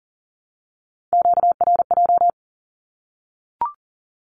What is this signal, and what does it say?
Morse code, heard as a keyed tone, about 40 wpm, 700 Hz. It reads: QRJ
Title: Q-Codes Repeated 3x 40wpm